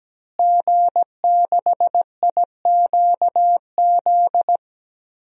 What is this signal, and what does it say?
Morse code, heard as a keyed tone, about 17 wpm, 700 Hz. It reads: G6IQZ
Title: DXCC-100 Call Signs - 5 Characters - Repeated 2x - 17wpm